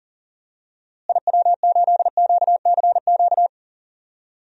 Morse code, heard as a keyed tone, about 37 wpm, 700 Hz. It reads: IW8QCQ
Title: DXCC-100 Call Signs - 6 Characters - Repeated 3x - 40wpm